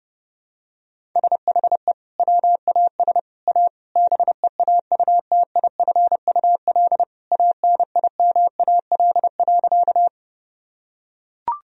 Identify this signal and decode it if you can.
Morse code, heard as a keyed tone, about 30 wpm, 700 Hz. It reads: SHE WAS A BEAUTIFUL ANIMAL.
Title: Sentences from Top 1000 Words - Review Part 1 - 30wpm